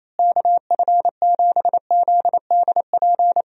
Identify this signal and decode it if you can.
Morse code, heard as a keyed tone, about 28 wpm, 700 Hz. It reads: KF7ZDP
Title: USA Call Signs - 6 Characters - Repeated 3x - 28wpm